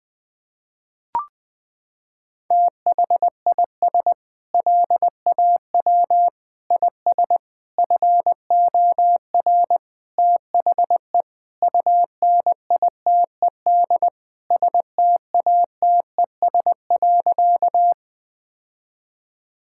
Morse code, heard as a keyed tone, about 20 wpm, 700 Hz. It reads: THIS LAW IS FOR THE UNITED STATES.
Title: Sentences with 4-Word Phrases - 20wpm